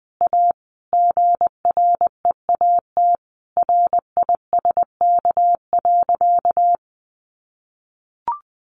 Morse code, heard as a keyed tone, about 20 wpm, 700 Hz. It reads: A GREAT RISK.
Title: Sentences from Top 600 Words 20wpm